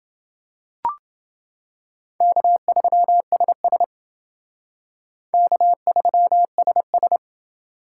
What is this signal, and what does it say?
Morse code, heard as a keyed tone, about 28 wpm, 700 Hz. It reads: K3SS K3SS